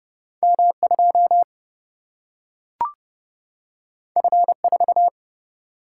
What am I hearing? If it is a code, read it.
Morse code, heard as a keyed tone, about 30 wpm, 700 Hz. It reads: M 2 F 4
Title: Single-Letter Single-Number 30wpm